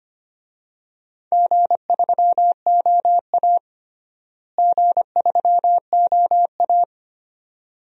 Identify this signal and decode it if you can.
Morse code, heard as a keyed tone, about 25 wpm, 700 Hz. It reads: G3OA G3OA
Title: DXCC-100 Call Signs - 4 Characters - Repeated 2x - 25wpm